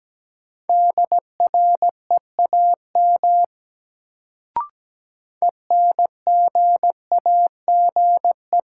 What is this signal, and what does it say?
Morse code, heard as a keyed tone, about 17 wpm, 700 Hz. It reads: DREAM ENGAGE
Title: Sentences from Top 1500 Words - New Word Drill Alt - 17wpm